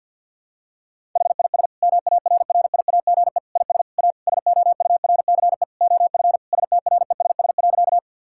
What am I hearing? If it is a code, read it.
Morse code, heard as a keyed tone, about 50 wpm, 700 Hz. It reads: HIS MARRIAGE IS A SOURCE OF STRESS.
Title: Sentences from Top 1000 Words - Review Part 6 - 50wpm